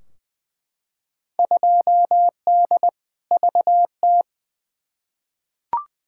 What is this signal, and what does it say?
Morse code, heard as a keyed tone, about 20 wpm, 700 Hz. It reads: 2D VT
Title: Field Day Exchanges Encore - 20wpm